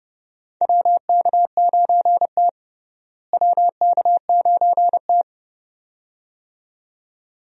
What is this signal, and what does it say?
Morse code, heard as a keyed tone, about 30 wpm, 700 Hz. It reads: WK9T WK9T
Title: USA Call Signs - 4 Characters - Repeated 2x - 30wpm